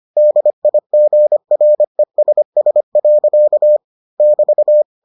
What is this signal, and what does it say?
Morse code, heard as a keyed tone, about 25 wpm, 600 Hz. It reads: DIGRESS. <BT>